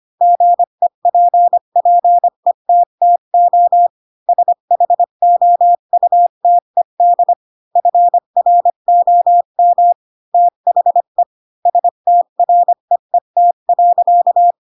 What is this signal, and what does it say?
Morse code, heard as a keyed tone, about 25 wpm, 700 Hz. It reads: GEPPETTO SHOUTED FROM THE STREET.
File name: Pinocchio / Pinocchio ch7 20 25 ik1hge